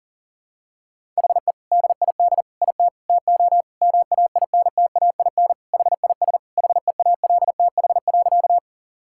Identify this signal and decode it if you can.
Morse code, heard as a keyed tone, about 40 wpm, 700 Hz. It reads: HE DID IT TO MAINTAIN HIS HEALTH.
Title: Sentences from Top 1000 Words - Review Part 5 - 40wpm